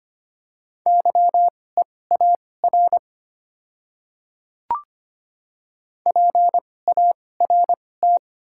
Morse code, heard as a keyed tone, about 25 wpm, spaced as for 18 wpm, 690 Hz. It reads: YEAR PART
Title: 2x Character Spacing - Top 200 Words - 25wpm